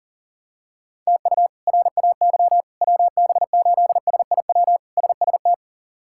Morse code, heard as a keyed tone, about 40 wpm, 700 Hz. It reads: TU RAY WB8SIW SST
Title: K1USN SST QSO Practice - 40wpm